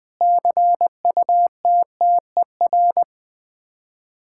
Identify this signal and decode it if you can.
Morse code, heard as a keyed tone, about 21 wpm, 690 Hz. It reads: CUTTER
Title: Difficult Words with Hints - 20wpm